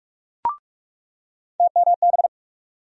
Morse code, heard as a keyed tone, about 43 wpm, 700 Hz. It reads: TMB
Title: USA Call Sign Postfixes - 3 Letters 45wpm